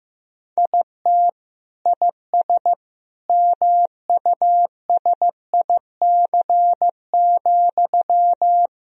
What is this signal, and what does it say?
Morse code, heard as a keyed tone, about 15 wpm, 700 Hz. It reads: IT IS MUSIC,
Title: Sentences from Top 1000 Words - Review Part 1 - 15wpm